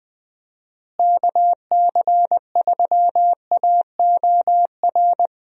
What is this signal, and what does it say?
Morse code, heard as a keyed tone, about 19 wpm, 700 Hz. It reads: KC3AOR
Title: USA Call Signs - 6 Characters - Repeated 2x - 20wpm